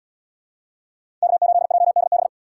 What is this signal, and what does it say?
Morse code, heard as a keyed tone, about 74 wpm, 700 Hz. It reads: K8JDB